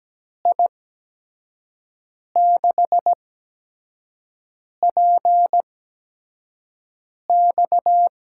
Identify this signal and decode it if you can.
Morse code, heard as a keyed tone, about 17 wpm, spaced as for 5 wpm, 700 Hz. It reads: I6PX
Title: USA Call Signs - 6 Characters - 8x Character Spacing - 17wpm